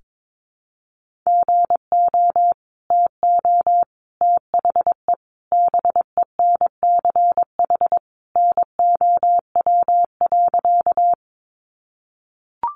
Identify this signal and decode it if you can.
Morse code, heard as a keyed tone, about 22 wpm, 700 Hz. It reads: GO TO THE BENCH NOW.
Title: Sentences from Top 1400 Words 22wpm